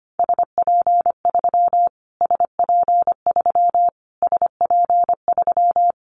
Morse code, heard as a keyed tone, about 25 wpm, 700 Hz. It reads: SP3 SP3 SP3
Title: Callsign Prefixes for Top 25 DXCC - Repeated 3x - 25wpm